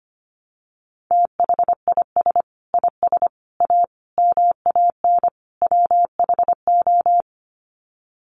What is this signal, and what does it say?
Morse code, heard as a keyed tone, about 25 wpm, 700 Hz. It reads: THIS IS A MAN WHO
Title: Sets of 5 Words Encore2 25wpm